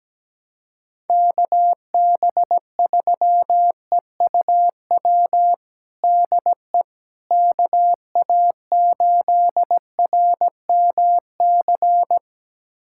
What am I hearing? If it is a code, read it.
Morse code, heard as a keyed tone, about 17 wpm, 700 Hz. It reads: KB3EUW DE KA8RMC
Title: USA Callsign DE Callsign - 6 Characters - 17wpm